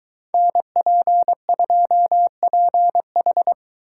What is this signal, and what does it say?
Morse code, heard as a keyed tone, about 23 wpm, 700 Hz. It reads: NP2PH